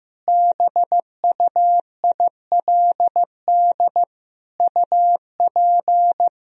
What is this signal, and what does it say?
Morse code, heard as a keyed tone, about 16 wpm, 700 Hz. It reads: BUILD UP
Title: Sentences from Top 1000 Words - Review Part 1 - 15wpm